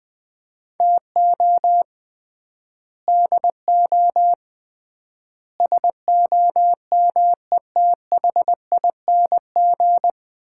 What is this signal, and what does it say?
Morse code, heard as a keyed tone, about 21 wpm, 700 Hz. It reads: TO DO SOMETHING
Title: 3x Word-Spacing - Sets of 5 Words - 20wpm